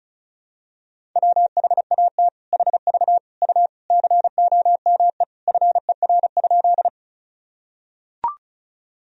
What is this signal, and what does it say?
Morse code, heard as a keyed tone, about 34 wpm, 700 Hz. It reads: WHAT HV U COME FER?
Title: Sentences from Top 100 Words and Abbreviations 35wpm